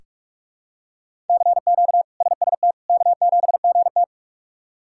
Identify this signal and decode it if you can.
Morse code, heard as a keyed tone, about 46 wpm, 700 Hz. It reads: CQ SST K7GT
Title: K1USN SST QSO Practice - 45wpm